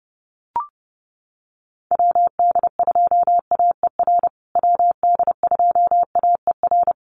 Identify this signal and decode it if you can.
Morse code, heard as a keyed tone, about 30 wpm, 700 Hz. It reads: WD2AER WD2AER
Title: USA Call Signs Repeated 3x 30wpm